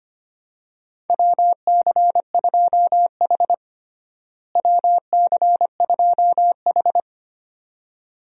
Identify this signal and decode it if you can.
Morse code, heard as a keyed tone, about 25 wpm, 700 Hz. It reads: WC2H WC2H
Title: USA Call Signs - 4 Characters - Repeated 2x - 25wpm